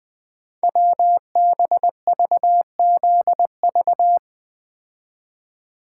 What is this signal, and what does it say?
Morse code, heard as a keyed tone, about 20 wpm, 700 Hz. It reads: WBVZV
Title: Code Groups - 5 Letters - 20wpm